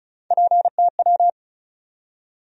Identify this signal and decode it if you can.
Morse code, heard as a keyed tone, about 36 wpm, 700 Hz. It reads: PTW
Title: USA Call Sign Postfixes - 3 Letters 35wpm